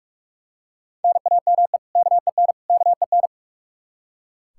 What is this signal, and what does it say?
Morse code, heard as a keyed tone, about 44 wpm, 700 Hz. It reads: NAME KEN KEN
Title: Practice QSO - Exchange 1 - Beginning - 45wpm